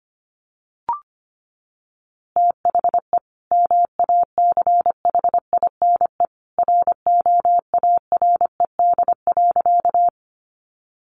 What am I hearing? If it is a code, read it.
Morse code, heard as a keyed tone, about 25 wpm, 700 Hz. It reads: THE MACHINE ROARED.
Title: A Sound Of Thunder 25wpm